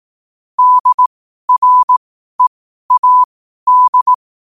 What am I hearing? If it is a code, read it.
Morse code, heard as a keyed tone, about 18 wpm, spaced as for 12 wpm, 1000 Hz. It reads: DREAD
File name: qotdfiles / 12wpm